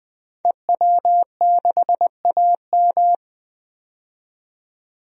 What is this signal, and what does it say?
Morse code, heard as a keyed tone, about 20 wpm, 700 Hz. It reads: EW6AM
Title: DXCC-100 Call Signs - 5 Characters - 20wpm